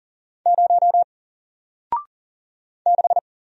Morse code, heard as a keyed tone, about 40 wpm, 700 Hz. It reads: 0 6
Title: Single Character (Letter:Number:Punctuation:Prosign) Alt 40wpm